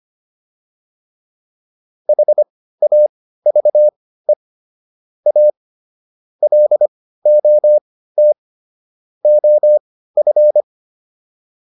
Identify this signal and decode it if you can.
Morse code, heard as a keyed tone, about 25 wpm, spaced as for 15 wpm, 600 Hz. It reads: HAVE A LOT OF